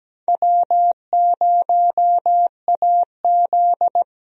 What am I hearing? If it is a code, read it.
Morse code, heard as a keyed tone, about 17 wpm, 700 Hz. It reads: W0AZ
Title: USA Call Signs - 4 Characters - Repeated 3x - 17wpm